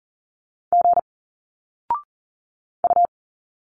Morse code, heard as a keyed tone, about 40 wpm, 700 Hz. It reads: G U
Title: Single Letters 40wpm